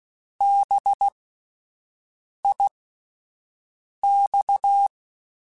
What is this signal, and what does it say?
Morse code, heard as a keyed tone, about 16 wpm, spaced as for 5 wpm, 800 Hz. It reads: BIX
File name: Alphabetical / Alphabetical mp3s / Bs @5